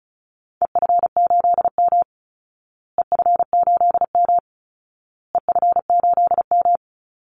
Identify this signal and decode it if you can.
Morse code, heard as a keyed tone, about 34 wpm, 700 Hz. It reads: EF8M EF8M EF8M
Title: Call Signs Repeated 2x - 35wpm